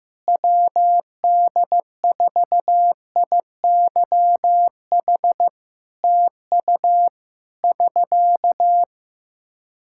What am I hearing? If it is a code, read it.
Morse code, heard as a keyed tone, about 15 wpm, 700 Hz. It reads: WD4IYH TU <SK>
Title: Practice QSO - Exchange 4 - Ending - 15wpm